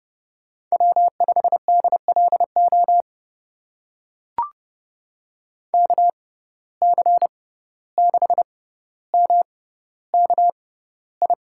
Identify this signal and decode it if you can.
Morse code, heard as a keyed tone, about 32 wpm, spaced as for 15 wpm, 700 Hz. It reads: W5DLO KC6MKI
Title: USA Call Signs - 6x Character Spacing - 30wpm